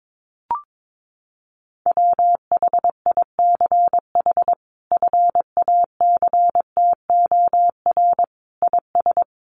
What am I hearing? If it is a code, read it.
Morse code, heard as a keyed tone, about 22 wpm, 700 Hz. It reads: WHICH FACTOR IS
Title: Sentences from Top 1300 Words 22wpm